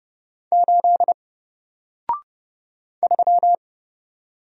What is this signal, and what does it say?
Morse code, heard as a keyed tone, about 30 wpm, 700 Hz. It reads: 8 3
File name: Single-Letter-Number-30wpm